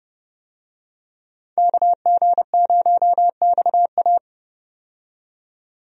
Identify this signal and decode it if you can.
Morse code, heard as a keyed tone, about 31 wpm, 700 Hz. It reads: KG0XA
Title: USA Call Signs - 5 Characters - 30wpm